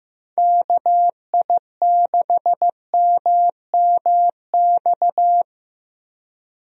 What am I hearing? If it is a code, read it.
Morse code, heard as a keyed tone, about 15 wpm, 700 Hz. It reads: KI6MMX
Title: USA Call Signs - 6 Characters - Repeated 3x - 15wpm